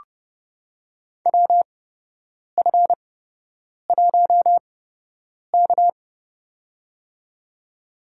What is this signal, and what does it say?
Morse code, heard as a keyed tone, about 30 wpm, spaced as for 8 wpm, 700 Hz. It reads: WF1K
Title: USA Call Signs - 8x Character Spacing - 30wpm